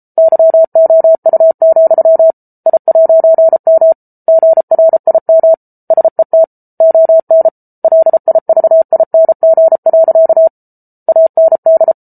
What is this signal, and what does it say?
Morse code, heard as a keyed tone, about 34 wpm, 650 Hz. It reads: YOU, I'M GRIM SET ON LIVING. AND